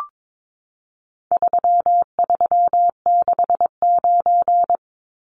Morse code, heard as a keyed tone, about 22 wpm, 700 Hz. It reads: 3369
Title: Numbers - 4-Digit 22wpm